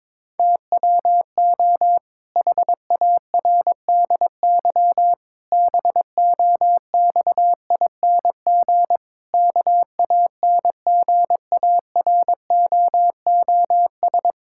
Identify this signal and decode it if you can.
Morse code, heard as a keyed tone, about 22 wpm, 700 Hz. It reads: TWO HARDY BOXING KANGAROOS
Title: Pangram Challenge v2 - 22wpm